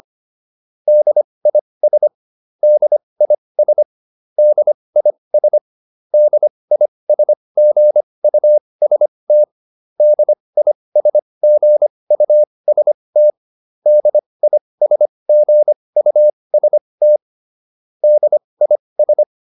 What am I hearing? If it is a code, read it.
Morse code, heard as a keyed tone, about 25 wpm, spaced as for 20 wpm, 600 Hz. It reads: DIS DIS DIS DISGUST DISGUST DISGUST DIS
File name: DIS7 20